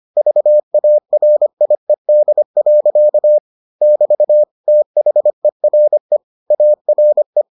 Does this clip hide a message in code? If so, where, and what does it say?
Yes: Morse code from 0.2 to 7.4 s, heard as a keyed tone, about 25 wpm, 600 Hz. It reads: VARIED. <BT> THERE ARE